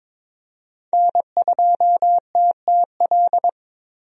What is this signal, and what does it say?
Morse code, heard as a keyed tone, about 22 wpm, 700 Hz. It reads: N2TTL
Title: USA Call Signs - 5 Characters - 22wpm